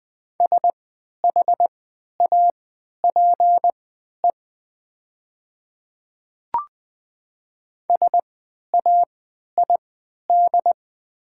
Morse code, heard as a keyed tone, about 20 wpm, spaced as for 11 wpm, 700 Hz. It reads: SHAPE SAID
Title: 3x Character Spacing - Top 500 Words - 20wpm